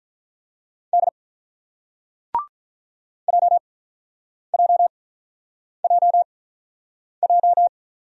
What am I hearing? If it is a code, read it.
Morse code, heard as a keyed tone, about 49 wpm, spaced as for 9 wpm, 700 Hz. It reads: D JJJJ